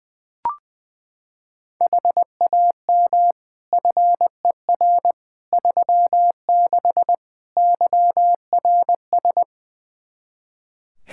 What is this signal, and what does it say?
Morse code, heard as a keyed tone, about 20 wpm, 700 Hz. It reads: HAM FER 36 YRS
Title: Practice QSO - Exchange 3 - More Info - 20wpm